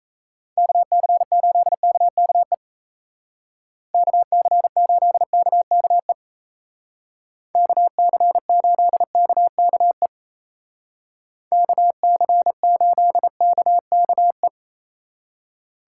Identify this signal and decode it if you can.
Morse code, heard as a keyed tone, about 33 wpm, 700 Hz. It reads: KC8KKE KC8KKE KC8KKE KC8KKE
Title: Speed Racing - USA 2x3 Call Signs 28wpm